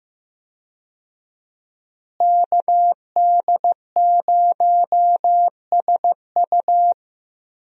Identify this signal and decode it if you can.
Morse code, heard as a keyed tone, about 15 wpm, 700 Hz. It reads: KD0SU
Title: USA Call Signs - 5 Characters - Repeated 2x - 15wpm